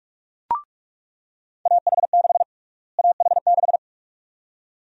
Morse code, heard as a keyed tone, about 46 wpm, 700 Hz. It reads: AH6 AH6
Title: USA Call Sign Prefixes - Repeated 2x - 45wpm